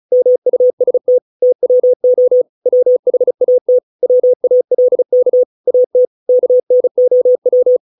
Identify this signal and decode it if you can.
Morse code, heard as a keyed tone, about 35 wpm, 500 Hz. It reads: MUST TWO WHAT WALK AT KNOW